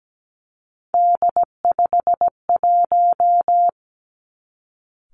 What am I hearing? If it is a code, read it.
Morse code, heard as a keyed tone, about 17 wpm, 700 Hz. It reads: D51
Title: Three Challenging Letter-Number Combinations 17wpm